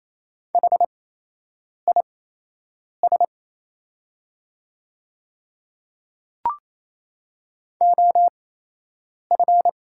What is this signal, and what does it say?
Morse code, heard as a keyed tone, about 27 wpm, spaced as for 8 wpm, 690 Hz. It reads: HIS OF
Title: 8x Character Spacing - Top 200 Words - 28wpm